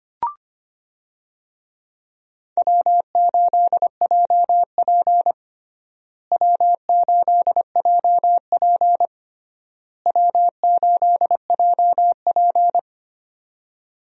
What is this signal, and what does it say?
Morse code, heard as a keyed tone, about 25 wpm, 700 Hz. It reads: W8JP W8JP W8JP
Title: USA Call Signs - 4 Characters - Repeated 3x - 25wpm